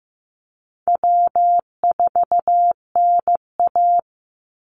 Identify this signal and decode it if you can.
Morse code, heard as a keyed tone, about 15 wpm, 700 Hz. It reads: W4NA
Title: Easy Call Signs - 15wpm